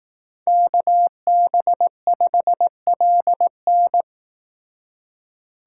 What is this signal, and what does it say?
Morse code, heard as a keyed tone, about 18 wpm, 700 Hz. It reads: KB5LN
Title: Speed Racing - USA 2x2 Call Signs 15wpm